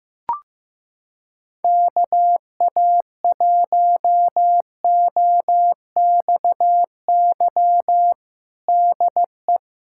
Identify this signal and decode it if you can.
Morse code, heard as a keyed tone, about 15 wpm, 700 Hz. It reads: KA1OXY DE
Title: USA Callsign DE Callsign - 6 Characters - 15wpm